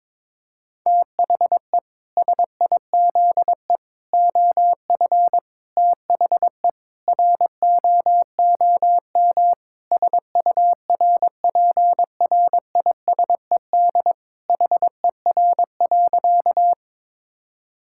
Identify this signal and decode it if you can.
Morse code, heard as a keyed tone, about 22 wpm, 700 Hz. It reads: THE SIZE OF THE ROOM SURPRISED HER.
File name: practice / 22wpm / Sentences with 4-Word Phrases - 22wpm